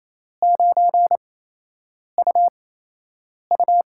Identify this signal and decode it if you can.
Morse code, heard as a keyed tone, about 28 wpm, spaced as for 8 wpm, 700 Hz. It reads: 9UU
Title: USA Call Signs - 4 Characters - 8x Character Spacing - 28wpm